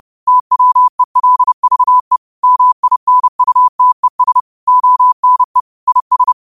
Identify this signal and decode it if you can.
Morse code, heard as a keyed tone, about 30 wpm, 1000 Hz. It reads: TWELVE MINUTES ONE IS
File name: 30wpm